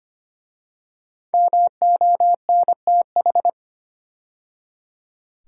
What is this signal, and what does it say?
Morse code, heard as a keyed tone, about 25 wpm, 700 Hz. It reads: MONTH
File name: 5 Letter Words - First 3 Letters Given - 25wpm